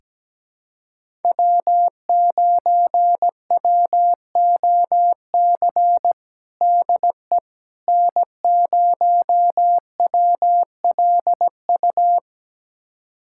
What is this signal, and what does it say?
Morse code, heard as a keyed tone, about 17 wpm, 700 Hz. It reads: W9WOC DE N0WLU
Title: USA Callsign DE Callsign - 5 Characters - 17wpm